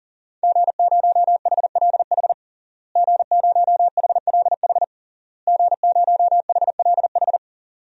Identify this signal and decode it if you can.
Morse code, heard as a keyed tone, about 40 wpm, 700 Hz. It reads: G0HLH G0HLH G0HLH
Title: DXCC-100 Call Signs - 5 Characters - Repeated 3x - 40wpm